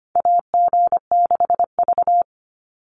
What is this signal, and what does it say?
Morse code, heard as a keyed tone, about 25 wpm, 700 Hz. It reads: AG6V
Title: Call Signs - 25wpm